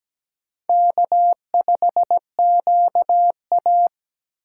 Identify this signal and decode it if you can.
Morse code, heard as a keyed tone, about 17 wpm, 700 Hz. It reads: K5QA
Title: USA Call Signs - 4 Characters - 2x Character Spacing - 17wpm